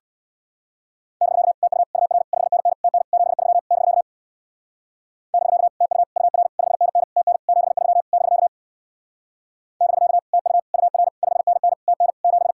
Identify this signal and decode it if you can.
Morse code, heard as a keyed tone, about 62 wpm, 700 Hz. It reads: <BK> TU UR 5NN TN 73 <BK> <BK> TU UR 5NN TN 73 <BK> <BK> TU UR 5NN TN 7
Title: Speed-Racing Common QSO Phrases - 50wpm